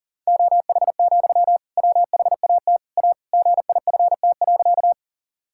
Morse code, heard as a keyed tone, about 40 wpm, 700 Hz. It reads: OH, WHAT A GIFT.
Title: Sentences from Top 1000 Words - Review Part 6 - 40wpm